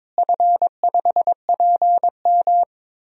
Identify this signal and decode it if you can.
Morse code, heard as a keyed tone, about 23 wpm, 700 Hz. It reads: F5PM
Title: DXCC-100 Call Signs - 4 Characters - Repeated 3x - 22wpm